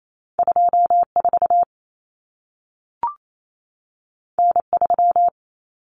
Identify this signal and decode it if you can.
Morse code, heard as a keyed tone, about 28 wpm, 700 Hz. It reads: 24 N3
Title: Common Cut Numbers - 2-Digit 28wpm